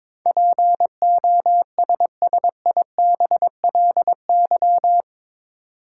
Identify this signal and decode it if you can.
Morse code, heard as a keyed tone, about 22 wpm, 700 Hz. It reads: POSSIBLY
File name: practice / 22wpm / Eight Letter Words - 22wpm